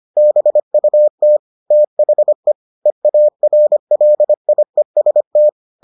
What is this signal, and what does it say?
Morse code, heard as a keyed tone, about 25 wpm, 600 Hz. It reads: BUT THE EARLIEST